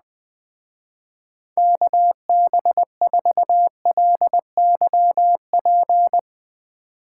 Morse code, heard as a keyed tone, about 21 wpm, 700 Hz. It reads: KB4LYP